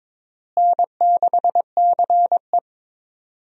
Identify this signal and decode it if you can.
Morse code, heard as a keyed tone, about 21 wpm, 700 Hz. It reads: N6CE
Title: USA Call Signs - 4 Characters - Repeated 3x - 22wpm